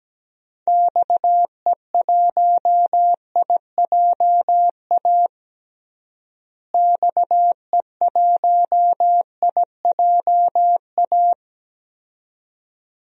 Morse code, heard as a keyed tone, about 17 wpm, 700 Hz. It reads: XE1IJA XE1IJA
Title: DXCC-100 Call Signs - 6 Characters - Repeated 2x - 17wpm